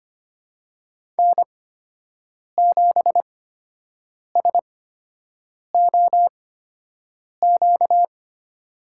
Morse code, heard as a keyed tone, about 25 wpm, spaced as for 7 wpm, 700 Hz. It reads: N7SOQ
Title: USA Call Signs - 8x Character Spacing - 25wpm